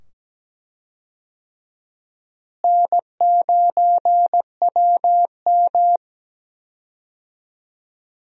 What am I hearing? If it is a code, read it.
Morse code, heard as a keyed tone, about 17 wpm, 700 Hz. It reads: N9WM
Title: USA Call Signs - 4 Characters - Repeated 2x - 17wpm